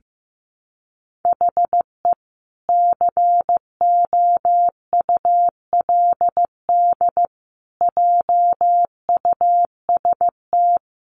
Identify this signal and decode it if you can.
Morse code, heard as a keyed tone, about 15 wpm, 700 Hz. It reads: HE COULD JUST
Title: Sentences from Top 1100 Words 15wpm